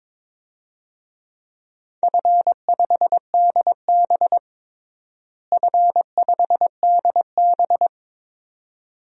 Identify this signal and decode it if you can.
Morse code, heard as a keyed tone, about 22 wpm, 700 Hz. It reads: F5DB F5DB
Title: DXCC-100 Call Signs - 4 Characters - Repeated 2x - 22wpm